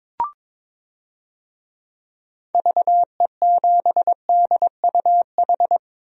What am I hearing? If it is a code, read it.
Morse code, heard as a keyed tone, about 22 wpm, 700 Hz. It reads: VE7DUH
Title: DXCC-100 Call Signs - 6 Characters - 22wpm